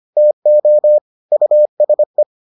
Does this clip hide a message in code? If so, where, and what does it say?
Yes: Morse code from 0.2 to 2.2 s, heard as a keyed tone, about 25 wpm, 600 Hz. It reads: TO USE